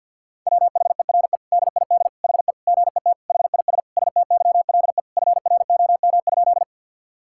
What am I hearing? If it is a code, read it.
Morse code, heard as a keyed tone, about 50 wpm, 700 Hz. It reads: WHERE DID HE GET HIS STYLE FROM?
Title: Sentences from Top 1000 Words - Review Part 5 - 50wpm